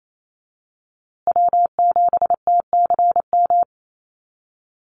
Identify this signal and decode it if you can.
Morse code, heard as a keyed tone, about 28 wpm, 700 Hz. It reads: W7TCM
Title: USA Call Signs Repeated 2x 28wpm